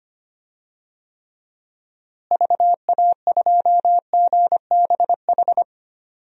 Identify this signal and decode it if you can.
Morse code, heard as a keyed tone, about 25 wpm, 700 Hz. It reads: VA2GBH
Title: DXCC-100 Call Signs - 6 Characters - Repeated 3x - 25wpm